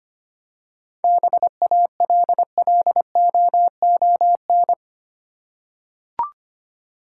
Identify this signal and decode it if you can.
Morse code, heard as a keyed tone, about 25 wpm, 700 Hz. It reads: BALLOON
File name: practice / 25wpm / Easy Words with Hints - 25wpm